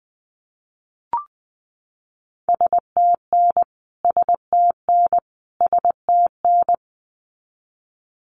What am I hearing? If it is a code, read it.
Morse code, heard as a keyed tone, about 20 wpm, 700 Hz. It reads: STN STN STN
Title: Abbreviations Repeated 3x - 20wpm